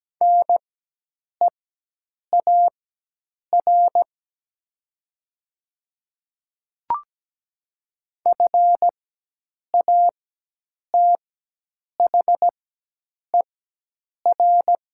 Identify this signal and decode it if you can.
Morse code, heard as a keyed tone, about 17 wpm, spaced as for 8 wpm, 700 Hz. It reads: NEAR FATHER
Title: 4x Character Spacing - Top 500 Words - 17wpm